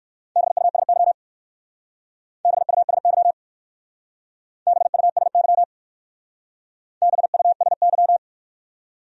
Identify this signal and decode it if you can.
Morse code, heard as a keyed tone, about 53 wpm, 700 Hz. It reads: BUSY BUSY BUSY BUSY
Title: Speed-Racing Top 500 Words 45wpm